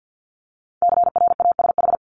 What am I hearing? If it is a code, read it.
Morse code, heard as a keyed tone, about 50 wpm, 700 Hz. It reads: CRASH
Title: Multi-letter Phonemes 50wpm